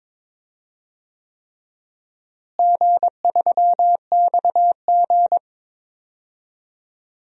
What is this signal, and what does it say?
Morse code, heard as a keyed tone, about 21 wpm, 700 Hz. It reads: G3XG
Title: DXCC-100 Call Signs - 4 Characters - Repeated 3x - 22wpm